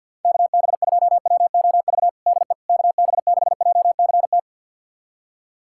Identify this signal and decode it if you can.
Morse code, heard as a keyed tone, about 52 wpm, 700 Hz. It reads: KB2WOV DE KB6JCT
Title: USA Callsign DE Callsign - 6 Characters - 50wpm